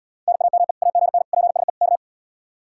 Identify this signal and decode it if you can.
Morse code, heard as a keyed tone, about 74 wpm, 700 Hz. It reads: NAME AGN PSE K